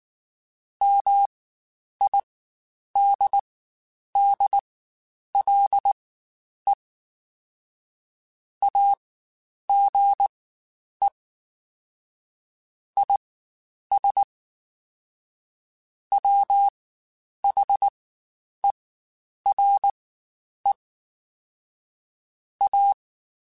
Morse code, heard as a keyed tone, about 19 wpm, spaced as for 9 wpm, 800 Hz. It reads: MIDDLE AGE IS WHERE A
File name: Spaced / Chuckles Spaced-mp3s / MiddleAge sp @18